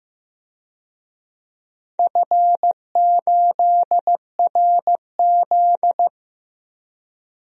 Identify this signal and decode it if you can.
Morse code, heard as a keyed tone, about 15 wpm, 700 Hz. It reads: F8RZ
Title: DXCC-100 Call Signs - 4 Characters - Repeated 3x - 15wpm